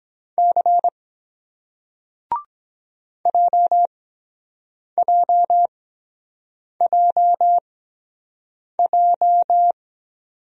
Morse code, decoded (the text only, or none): C JJJJ